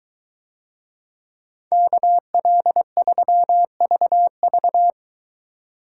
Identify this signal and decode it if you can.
Morse code, heard as a keyed tone, about 23 wpm, 700 Hz. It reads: KL3VV